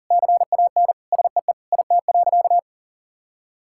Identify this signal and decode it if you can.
Morse code, heard as a keyed tone, about 37 wpm, 700 Hz. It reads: CAN SEE IT.
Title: Sentences from Top 1600 Words 40wpm